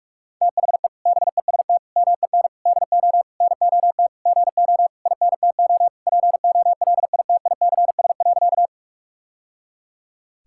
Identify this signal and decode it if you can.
Morse code, heard as a keyed tone, about 48 wpm, 700 Hz. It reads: THE BEST MEN DO NOT GO INTO POLITICS.
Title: Sentences from Top 1000 Words - Review Part 6 - 45wpm